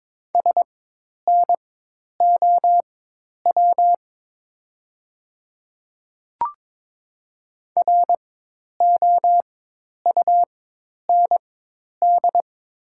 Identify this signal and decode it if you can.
Morse code, heard as a keyed tone, about 21 wpm, spaced as for 10 wpm, 700 Hz. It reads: SNOW ROUND
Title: 4x Character Spacing - Top 500 Words - 22wpm